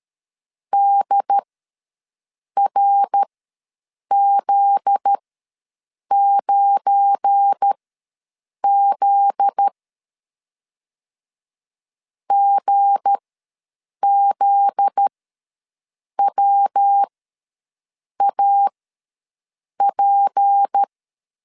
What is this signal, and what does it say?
Morse code, heard as a keyed tone, about 13 wpm, spaced as for 6 wpm, 800 Hz. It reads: DRZ9Z GZWAP
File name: LEZIONI / 7 GF9Z / 7 GF9Z ws 12wpm 3321